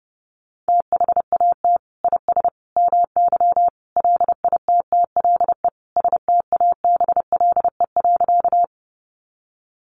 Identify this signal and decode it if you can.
Morse code, heard as a keyed tone, about 30 wpm, 700 Hz. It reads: THAT IS MY LITTLE STABLE.
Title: Sentences from Top 1500 Words 30wpm